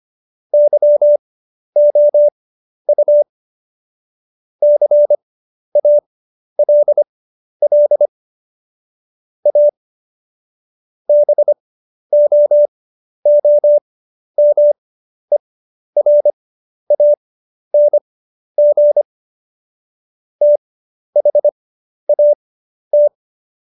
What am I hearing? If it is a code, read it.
Morse code, heard as a keyed tone, about 25 wpm, spaced as for 11 wpm, 590 Hz. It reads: YOU CALL A BOOMERANG THAT
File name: Jokes-25-char-10-eff